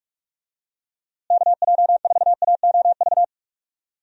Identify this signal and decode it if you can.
Morse code, heard as a keyed tone, about 46 wpm, 700 Hz. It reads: KJ4AOV
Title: USA Call Signs - 6 Characters - Repeated 2x - 45wpm